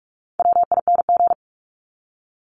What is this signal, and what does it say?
Morse code, heard as a keyed tone, about 44 wpm, 700 Hz. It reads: WING
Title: Top 1001-1250 Words 45wpm